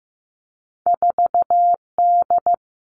Morse code, heard as a keyed tone, about 15 wpm, 700 Hz. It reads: 4D
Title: Two Challenging Letter-Number Combinations 15wpm